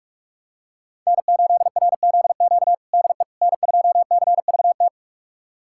Morse code, heard as a keyed tone, about 47 wpm, 700 Hz. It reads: N8RZQ DE N2CVT
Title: USA Callsign DE Callsign - 5 Characters - 45wpm